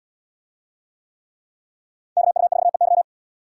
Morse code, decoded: KU4EY

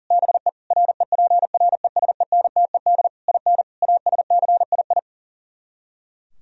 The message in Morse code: BE REPRESENTED IN ASCII

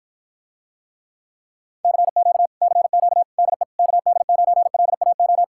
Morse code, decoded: CQ CQ DE KD9LAO

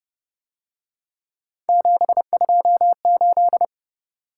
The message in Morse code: 728

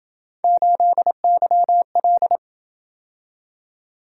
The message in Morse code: 8YL

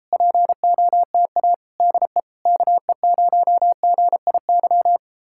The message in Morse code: POTA DE KE0GIY

